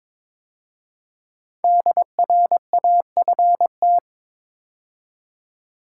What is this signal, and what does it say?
Morse code, heard as a keyed tone, about 22 wpm, 700 Hz. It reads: DRAFT
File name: 5 Letter Words - First 3 Letters Given - 22wpm